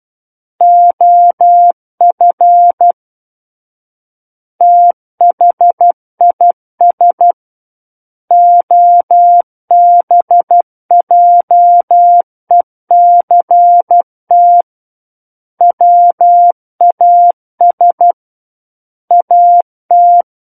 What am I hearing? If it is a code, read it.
Morse code, heard as a keyed tone, about 12 wpm, 700 Hz. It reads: OF THIS OBJECT WAS AT